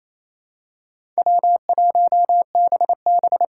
Morse code, heard as a keyed tone, about 28 wpm, 700 Hz. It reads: W1BB
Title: USA Call Signs - 4 Characters - 3x Character Spacing - 28wpm